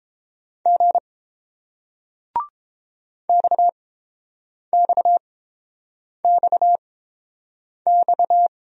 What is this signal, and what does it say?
Morse code, heard as a keyed tone, about 29 wpm, spaced as for 7 wpm, 700 Hz. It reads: G XXXX